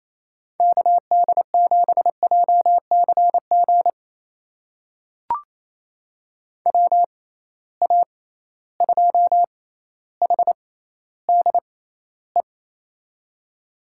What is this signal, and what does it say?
Morse code, heard as a keyed tone, about 28 wpm, spaced as for 14 wpm, 700 Hz. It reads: KD7JCG WA2HDE
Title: USA Call Signs - 6 Characters - 6x Character Spacing - 28wpm